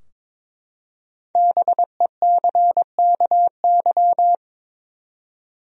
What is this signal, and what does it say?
Morse code, heard as a keyed tone, about 22 wpm, 700 Hz. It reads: BECKY
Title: USA Common Names - Max 5 Letters - 22wpm